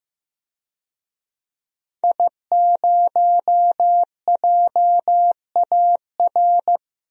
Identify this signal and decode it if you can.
Morse code, heard as a keyed tone, about 15 wpm, 700 Hz. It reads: I0JAR
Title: DXCC-100 Call Signs - 5 Characters - Repeated 3x - 15wpm